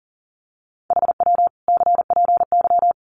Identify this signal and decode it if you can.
Morse code, heard as a keyed tone, about 40 wpm, 700 Hz. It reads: HW CPY
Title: Common QSO Phrases 40wpm